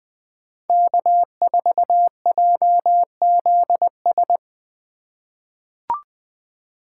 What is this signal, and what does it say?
Morse code, heard as a keyed tone, about 20 wpm, 700 Hz. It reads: K4JZS
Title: USA Call Signs - 5 Characters - 2x Character Spacing - 20wpm